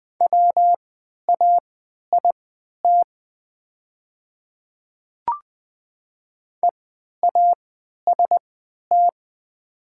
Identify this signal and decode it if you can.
Morse code, heard as a keyed tone, about 20 wpm, spaced as for 11 wpm, 700 Hz. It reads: WAIT EAST
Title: 3x Character Spacing - Top 500 Words - 20wpm